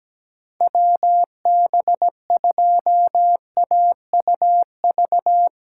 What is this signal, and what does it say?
Morse code, heard as a keyed tone, about 17 wpm, 700 Hz. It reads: WB2AUV
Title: USA Callsign DE Callsign - 6 Characters - 17wpm